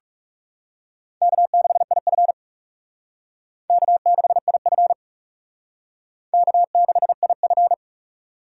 Morse code, heard as a keyed tone, about 37 wpm, 700 Hz. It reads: K6IF K6IF K6IF
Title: Speed Racing - USA Call Signs 30wpm